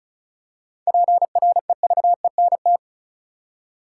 Morse code, heard as a keyed tone, about 36 wpm, 700 Hz. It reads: PREVENT
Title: Seven Letter Words - 35wpm